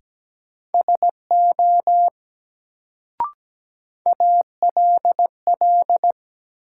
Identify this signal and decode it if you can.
Morse code, heard as a keyed tone, about 17 wpm, 700 Hz. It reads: SO ALL
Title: Sets of 2 Words - New Word Drill - 17wpm